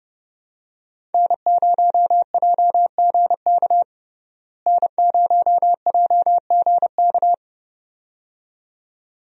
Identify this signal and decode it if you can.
Morse code, heard as a keyed tone, about 30 wpm, 700 Hz. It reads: N0JGK N0JGK
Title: USA Call Signs - 5 Characters - Repeated 2x - 30wpm